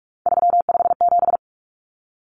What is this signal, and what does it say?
Morse code, heard as a keyed tone, about 44 wpm, 700 Hz. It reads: 357
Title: Common Cut Numbers - 3-Digit 45wpm